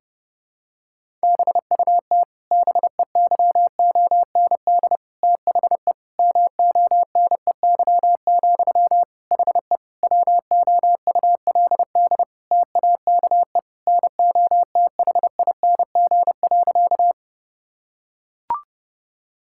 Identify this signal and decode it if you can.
Morse code, heard as a keyed tone, about 32 wpm, 700 Hz. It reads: BUT BEYOND THE MONEY, HE WOULD TAKE NOTHING.
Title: Sentences from Top 1000 Words - Review Part 6 - 30wpm